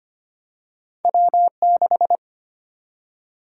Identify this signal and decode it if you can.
Morse code, heard as a keyed tone, about 25 wpm, 700 Hz. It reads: W6
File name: Two Letter-Number Combinations - 25wpm